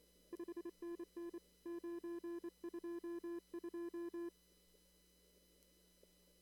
Morse code, heard as a keyed tone, about 28 wpm, 340 Hz. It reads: 5NN 922
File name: CW / kt5j wpx cw-op-k5pi